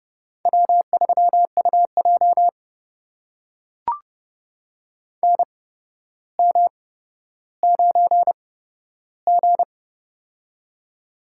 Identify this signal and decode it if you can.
Morse code, heard as a keyed tone, about 30 wpm, spaced as for 13 wpm, 700 Hz. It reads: W3UJ NM9G
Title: USA Call Signs - 4 Characters - 8x Character Spacing - 30wpm